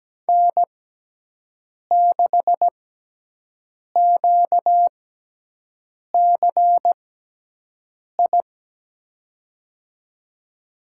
Morse code, heard as a keyed tone, about 17 wpm, spaced as for 6 wpm, 700 Hz. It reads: N6QCI
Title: USA Call Signs - 5 Characters - 6x Character Spacing - 17wpm